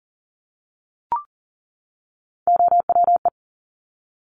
Morse code, heard as a keyed tone, about 38 wpm, 690 Hz. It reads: OWE
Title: Top Three Letter Combinations in Words 40wpm